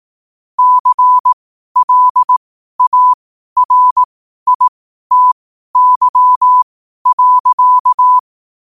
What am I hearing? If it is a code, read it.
Morse code, heard as a keyed tone, about 18 wpm, spaced as for 12 wpm, 1000 Hz. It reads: CLARITY.
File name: qotdfiles / 12wpm